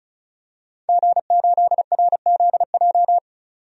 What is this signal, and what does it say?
Morse code, heard as a keyed tone, about 36 wpm, 700 Hz. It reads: G8RZJ
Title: DXCC-100 Call Signs - 5 Characters - 35wpm